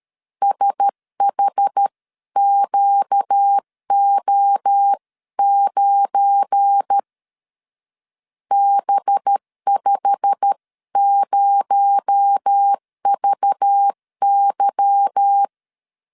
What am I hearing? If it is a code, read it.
Morse code, heard as a keyed tone, about 13 wpm, 800 Hz. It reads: SHQO9 B50VY
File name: LEZIONI / 10 SIGNS / 10B INT 12wpm 2414